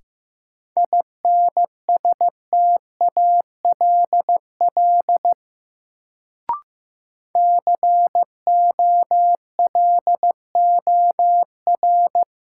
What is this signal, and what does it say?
Morse code, heard as a keyed tone, about 15 wpm, 700 Hz. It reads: INSTALL COLOR
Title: Sentences from Top 1600 Words - New Word Drill Alt - 15wpm